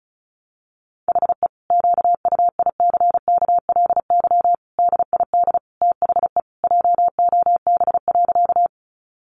Morse code, heard as a keyed tone, about 34 wpm, 700 Hz. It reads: HE QUICKLY DID THE JOB.
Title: Sentences from Top 700 Words 35wpm